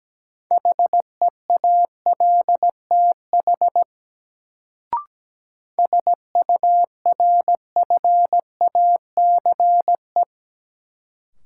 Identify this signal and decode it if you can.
Morse code, heard as a keyed tone, about 17 wpm, 700 Hz. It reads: HEALTH SURFACE
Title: Sentences from Top 600 Words - New Word Drill - 17wpm